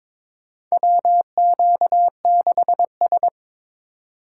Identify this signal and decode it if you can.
Morse code, heard as a keyed tone, about 22 wpm, 700 Hz. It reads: WQ6S
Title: USA Call Signs - 4 Characters - 4x Character Spacing - 22wpm